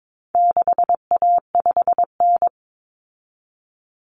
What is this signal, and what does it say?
Morse code, heard as a keyed tone, about 22 wpm, 700 Hz. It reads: 6A5N
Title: Common Cut Numbers - 4-Digit 22wpm